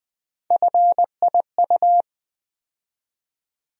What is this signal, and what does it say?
Morse code, heard as a keyed tone, about 20 wpm, 700 Hz. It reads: FIU
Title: USA Call Sign Postfixes - 3 Letters 20wpm